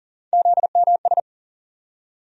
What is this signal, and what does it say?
Morse code, heard as a keyed tone, about 38 wpm, 700 Hz. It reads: ZMS
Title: USA Call Sign Postfixes - 3 Letters 40wpm